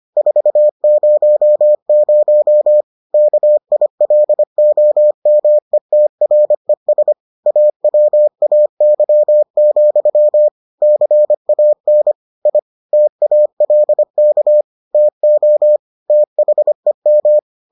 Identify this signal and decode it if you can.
Morse code, heard as a keyed tone, about 25 wpm, 600 Hz. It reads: 400 KILOMETRES AWAY, CAN I TALK TO THEM